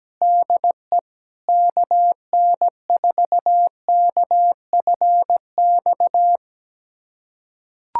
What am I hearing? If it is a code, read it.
Morse code, heard as a keyed tone, about 17 wpm, 700 Hz. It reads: DE KN4KFX
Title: USA Callsign DE Callsign - 6 Characters - 17wpm